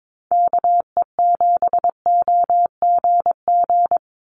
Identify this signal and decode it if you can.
Morse code, heard as a keyed tone, about 22 wpm, 700 Hz. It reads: KE7OGG
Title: USA Call Signs 22wpm